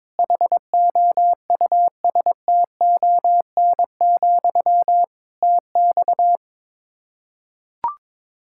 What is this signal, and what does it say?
Morse code, heard as a keyed tone, about 22 wpm, 700 Hz. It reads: HOUSTON, TX
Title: K1USN SST QSO Practice 22wpm